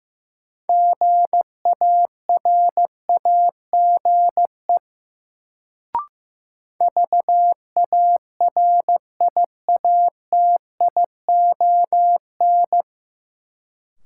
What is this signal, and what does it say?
Morse code, heard as a keyed tone, about 15 wpm, 700 Hz. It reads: GARAGE VARIATION
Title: Sentences from Top 1600 Words - New Word Drill - 15wpm